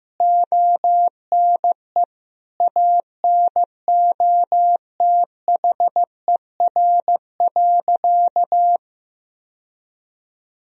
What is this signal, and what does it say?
Morse code, heard as a keyed tone, about 15 wpm, 700 Hz. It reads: ONE ANOTHER.
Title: Sentences from Top 1000 Words - Review Part 1 - 15wpm